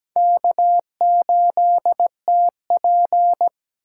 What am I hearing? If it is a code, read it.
Morse code, heard as a keyed tone, about 17 wpm, 700 Hz. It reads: K8TP
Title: USA Call Signs - 4 Characters - 6x Character Spacing - 17wpm